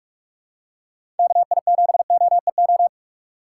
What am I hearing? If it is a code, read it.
Morse code, heard as a keyed tone, about 45 wpm, 700 Hz. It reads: KI7OEO
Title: USA Call Signs - 6 Characters - 6x Character Spacing - 45wpm